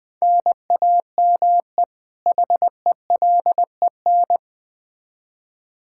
Morse code, heard as a keyed tone, about 20 wpm, 700 Hz. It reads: NAME HELEN
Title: Practice QSO - Exchange 1 - Beginning - 20wpm